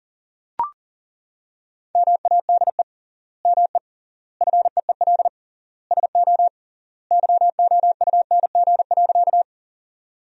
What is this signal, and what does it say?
Morse code, heard as a keyed tone, about 40 wpm, 700 Hz. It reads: MADE ME FEEL SO YOUNG.
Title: 3x Word-Spacing - Sentences from Top 400 Words - 40wpm